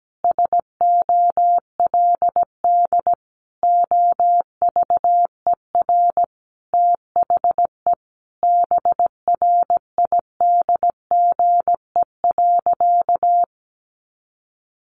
Morse code, heard as a keyed tone, about 17 wpm, 700 Hz. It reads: SOLD OVER THE BRIDGE.
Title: Sentences from Top 1400 Words 17wpm